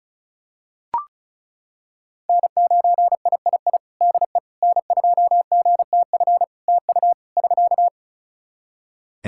Morse code, heard as a keyed tone, about 34 wpm, 700 Hz. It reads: N9III DE N2GTF TU <SK>
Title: Practice QSO - Exchange 4 - Ending - 35wpm